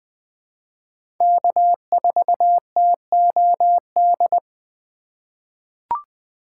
Morse code, heard as a keyed tone, about 20 wpm, 700 Hz. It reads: K4TOD
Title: USA Call Signs - 5 Characters - 3x Character Spacing - 20wpm